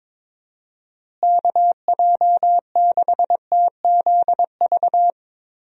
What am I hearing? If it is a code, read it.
Morse code, heard as a keyed tone, about 22 wpm, 700 Hz. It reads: KJ6TZV
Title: USA Call Signs - 6 Characters - Repeated 2x - 22wpm